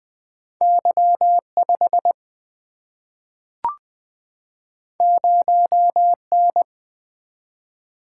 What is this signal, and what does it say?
Morse code, heard as a keyed tone, about 21 wpm, 700 Hz. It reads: Y 5 0 N
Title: Single-Letter Single-Number 20wpm